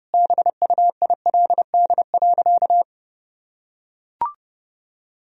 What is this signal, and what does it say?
Morse code, heard as a keyed tone, about 32 wpm, 700 Hz. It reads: BUILD.
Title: Sentences from Top 1000 Words - Review Part 1 - 30wpm